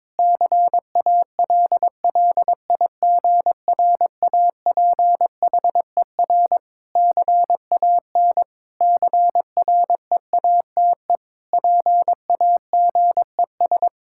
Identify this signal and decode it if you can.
Morse code, heard as a keyed tone, about 22 wpm, 700 Hz. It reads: CALLIGRAPHER CAN CREATE PAGES
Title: Pangram Challenge v2 - 22wpm